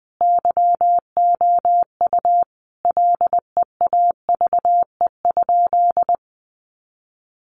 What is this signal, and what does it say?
Morse code, heard as a keyed tone, about 20 wpm, 700 Hz. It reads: YOU LEAVE?
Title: Sentences from Top 400 Words 20wpm